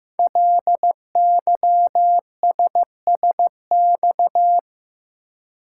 Morse code, heard as a keyed tone, about 15 wpm, 700 Hz. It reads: LYSSX
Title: Code Groups - 5 Letters - 15wpm